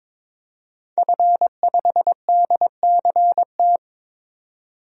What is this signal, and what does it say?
Morse code, heard as a keyed tone, about 22 wpm, 700 Hz. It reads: F5DCT
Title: DXCC-100 Call Signs - 5 Characters - 22wpm